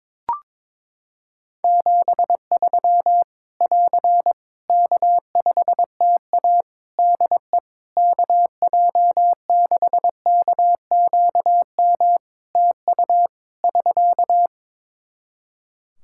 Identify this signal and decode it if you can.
Morse code, heard as a keyed tone, about 22 wpm, 700 Hz. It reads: 73 <AR> K5TA DE KJ6KQM TU <SK>
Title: Practice QSO - Exchange 4 - Ending - 22wpm